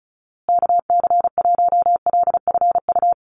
Morse code, heard as a keyed tone, about 34 wpm, 700 Hz. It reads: KC1LFU
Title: USA Call Signs Repeated 2x 35wpm